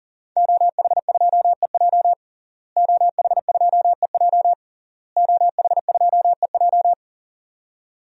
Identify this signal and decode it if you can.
Morse code, heard as a keyed tone, about 40 wpm, 700 Hz. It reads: OH2EJ OH2EJ OH2EJ
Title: DXCC-100 Call Signs - 5 Characters - Repeated 3x - 40wpm